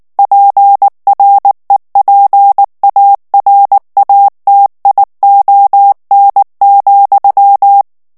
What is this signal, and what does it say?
Morse code, heard as a keyed tone, about 19 wpm, 800 Hz. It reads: PREPARATION,
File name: ANGLAIS texte clair 3 vitesse 18 mots minute